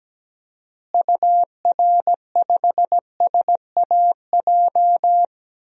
Morse code, heard as a keyed tone, about 17 wpm, 700 Hz. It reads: UR5SAJ
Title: DXCC-100 Call Signs - 6 Characters - Repeated 3x - 17wpm